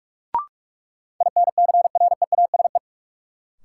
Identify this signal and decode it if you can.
Morse code, heard as a keyed tone, about 46 wpm, 700 Hz. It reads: INCREASE
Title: Sentences from Top 1200 Words - New Word Drill Alt - 45wpm